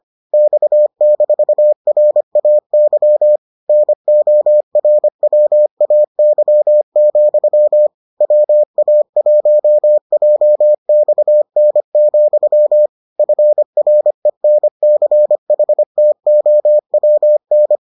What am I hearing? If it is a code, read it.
Morse code, heard as a keyed tone, about 25 wpm, 600 Hz. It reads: X-RAY NORWAY, WA1JXN, FRENCHTOWN